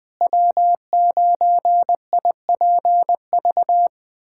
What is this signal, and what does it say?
Morse code, heard as a keyed tone, about 20 wpm, 700 Hz. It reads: W9IPV
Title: USA Callsign DE Callsign - 5 Characters - 20wpm